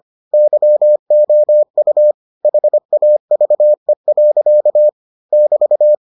Morse code, heard as a keyed tone, about 25 wpm, 600 Hz. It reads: YOU HAVE. <BT>